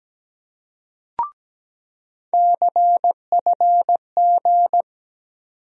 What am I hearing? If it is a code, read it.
Morse code, heard as a keyed tone, about 17 wpm, 700 Hz. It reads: CFG
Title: USA Call Sign Postfixes - 3 Letters 17wpm